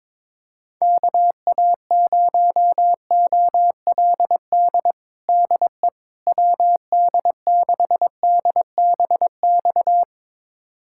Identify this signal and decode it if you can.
Morse code, heard as a keyed tone, about 21 wpm, 700 Hz. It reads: KA0OLD DE WD6DBX
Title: USA Callsign DE Callsign - 6 Characters - 22wpm